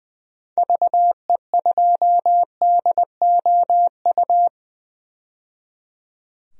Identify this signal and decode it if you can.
Morse code, heard as a keyed tone, about 20 wpm, 700 Hz. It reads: VE2DOU
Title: DXCC-100 Call Signs - 6 Characters - 20wpm